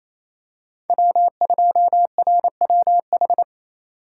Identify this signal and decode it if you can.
Morse code, heard as a keyed tone, about 27 wpm, 700 Hz. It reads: W2RWH
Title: USA Call Signs - 5 Characters - 4x Character Spacing - 28wpm